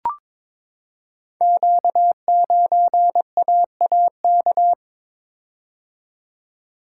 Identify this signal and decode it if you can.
Morse code, heard as a keyed tone, about 22 wpm, 700 Hz. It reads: Q9AAK
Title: Code Groups - 5 Letter:Numbers - 22wpm